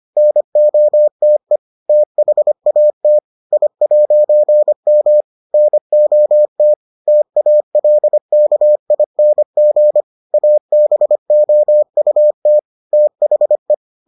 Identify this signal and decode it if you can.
Morse code, heard as a keyed tone, about 25 wpm, 600 Hz. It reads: NOTE THAT I'M NOT TALKING ABOUT THE